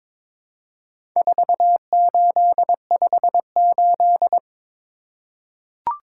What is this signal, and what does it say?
Morse code, heard as a keyed tone, about 22 wpm, 700 Hz. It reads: 4858
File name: practice / 22wpm / Four Letter-Number Combinations - 22wpm